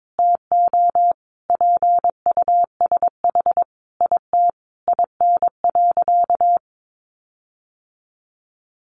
Morse code, heard as a keyed tone, about 22 wpm, 700 Hz. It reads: TO PUSH IT IN.
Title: Sentences from Top 1300 Words 22wpm